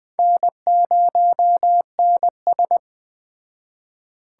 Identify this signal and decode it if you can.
Morse code, heard as a keyed tone, about 19 wpm, 700 Hz. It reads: N0NS